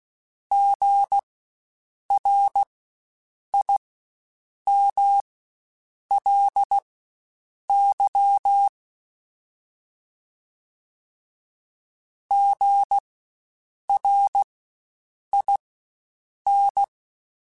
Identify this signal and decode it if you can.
Morse code, heard as a keyed tone, about 16 wpm, spaced as for 7 wpm, 800 Hz. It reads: GRIMLY GRIN
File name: Spaced Alphas / Spaced Alphas mp3s / Gs sp @15